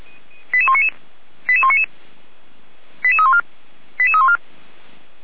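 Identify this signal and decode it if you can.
Five-tone selective call (ZVEI), heard as quick runs of five tones, tones of about 70 ms. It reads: 88199 88199 88314 88314
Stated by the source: ZVEI five-tone selective call received by SDR.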